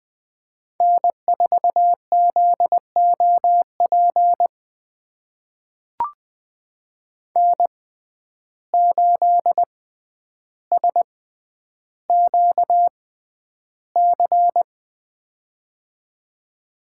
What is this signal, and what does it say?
Morse code, heard as a keyed tone, about 20 wpm, spaced as for 10 wpm, 700 Hz. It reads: N4ZOP N8SQC
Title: USA Call Signs - 5 Characters - 6x Character Spacing - 20wpm